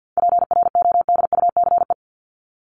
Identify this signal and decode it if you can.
Morse code, heard as a keyed tone, about 50 wpm, 700 Hz. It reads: PRODUCE
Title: Top 500 Words Repeated 50wpm